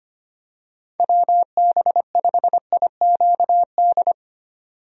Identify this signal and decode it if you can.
Morse code, heard as a keyed tone, about 25 wpm, 700 Hz. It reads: WB5IQD
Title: USA Call Signs - 6 Characters - 25wpm